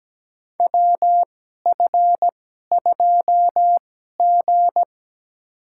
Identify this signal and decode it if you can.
Morse code, heard as a keyed tone, about 17 wpm, spaced as for 12 wpm, 700 Hz. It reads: WF2G
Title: USA Call Signs - 4 Characters - 2x Character Spacing - 17wpm